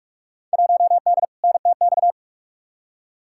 Morse code, heard as a keyed tone, about 46 wpm, 700 Hz. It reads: 1D NTX
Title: Field Day Exchanges - 45wpm